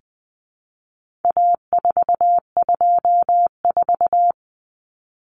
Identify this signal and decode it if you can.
Morse code, heard as a keyed tone, about 20 wpm, 700 Hz. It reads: A424
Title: Common Cut Numbers - 4-Digit 20wpm